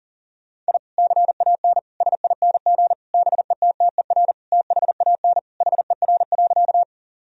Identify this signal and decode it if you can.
Morse code, heard as a keyed tone, about 37 wpm, 700 Hz. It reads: I CAN SING BETTER THAN HER.
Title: Sentences from Top 1000 Words - Review Part 2 - 40wpm